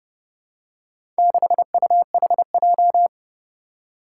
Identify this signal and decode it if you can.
Morse code, heard as a keyed tone, about 30 wpm, 700 Hz. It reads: 6UHJ
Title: Four Challenging Letter-Number Combinations 30wpm